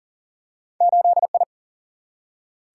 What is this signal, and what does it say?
Morse code, heard as a keyed tone, about 40 wpm, 700 Hz. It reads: 8I
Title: Single-Letter Single-Number 40wpm